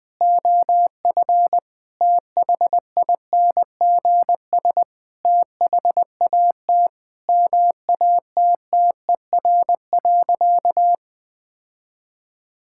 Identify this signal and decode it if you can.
Morse code, heard as a keyed tone, about 20 wpm, 700 Hz. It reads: OF THINGS THAT MATTER.
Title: Sentences with 5-Word Phrases - 20wpm